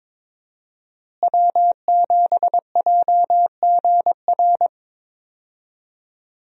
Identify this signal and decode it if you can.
Morse code, heard as a keyed tone, about 22 wpm, 700 Hz. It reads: W7JGR
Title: USA Call Signs - 5 Characters - Repeated 3x - 22wpm